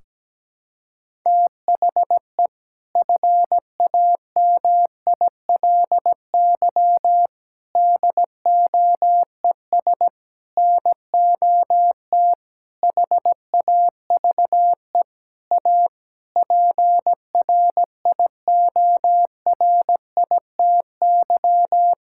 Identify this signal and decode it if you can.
Morse code, heard as a keyed tone, about 17 wpm, 700 Hz. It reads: THE FAMILY DOES NOT HAVE A PRIORITY
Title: Sentences from Top 1600 Words 17wpm